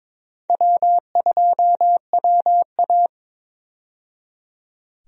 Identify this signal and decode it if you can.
Morse code, heard as a keyed tone, about 22 wpm, 700 Hz. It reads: W2WA
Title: USA Call Signs - 4 Characters - 22wpm